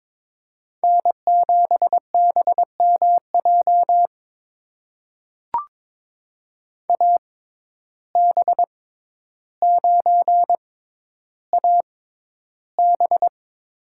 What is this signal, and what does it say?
Morse code, heard as a keyed tone, about 21 wpm, spaced as for 11 wpm, 700 Hz. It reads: N7BMJ AB9AB
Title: USA Call Signs - 5 Characters - 6x Character Spacing - 22wpm